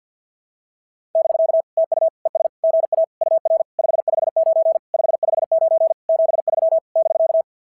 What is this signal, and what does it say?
Morse code, heard as a keyed tone, about 46 wpm, 660 Hz. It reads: <BK> TU ES GA UR 559 559 73 <BK>
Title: SOTA QSO Practice 50wpm